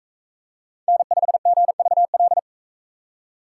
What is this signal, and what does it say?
Morse code, heard as a keyed tone, about 44 wpm, 700 Hz. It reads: N5GVL